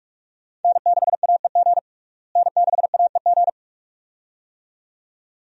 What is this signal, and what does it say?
Morse code, heard as a keyed tone, about 42 wpm, 700 Hz. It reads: N6AEG N6AEG
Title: USA Call Signs - 5 Characters - Repeated 2x - 45wpm